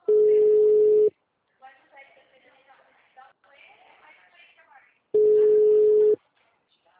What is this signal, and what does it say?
Ringback tone (Europe, 425 Hz), with speech.